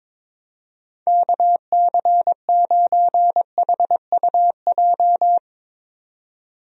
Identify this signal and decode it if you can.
Morse code, heard as a keyed tone, about 22 wpm, 700 Hz. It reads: KC9HUJ
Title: USA Call Signs - 6 Characters - 22wpm